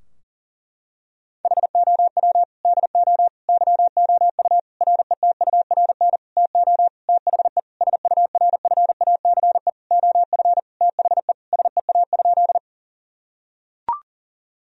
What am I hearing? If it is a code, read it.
Morse code, heard as a keyed tone, about 40 wpm, 700 Hz. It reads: HOW DO YOU RETURN TO THE SURFACE OF THE SEA?
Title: Sentences from Top 1000 Words - Review Part 3 - 40wpm